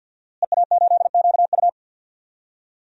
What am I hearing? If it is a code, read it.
Morse code, heard as a keyed tone, about 51 wpm, 700 Hz. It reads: EA8QU
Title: DXCC-100 Call Signs - 5 Characters - Repeated 2x - 50wpm